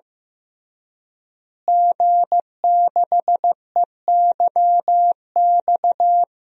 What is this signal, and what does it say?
Morse code, heard as a keyed tone, about 15 wpm, 700 Hz. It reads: G6EYX
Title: DXCC-100 Call Signs - 5 Characters - Repeated 3x - 15wpm